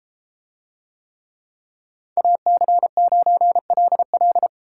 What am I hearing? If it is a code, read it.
Morse code, heard as a keyed tone, about 32 wpm, 700 Hz. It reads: AC9LL